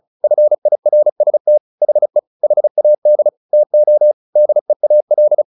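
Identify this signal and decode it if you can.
Morse code, heard as a keyed tone, about 34 wpm, 600 Hz. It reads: FIRST HE HAD TO DEAL
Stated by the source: Morse practice at 35 wpm from CW med Gnister, sending Ken Follett's novel Never in English.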